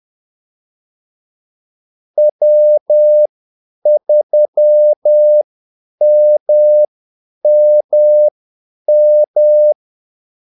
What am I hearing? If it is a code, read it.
Morse code, heard as a keyed tone, about 10 wpm, spaced as for 8 wpm, 600 Hz. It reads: W3MMM
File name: Callsigns-10-8wpm